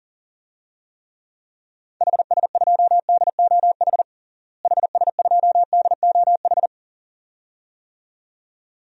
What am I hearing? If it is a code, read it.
Morse code, heard as a keyed tone, about 40 wpm, 700 Hz. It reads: HS2DOH HS2DOH
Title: DXCC-100 Call Signs - 6 Characters - Repeated 2x - 40wpm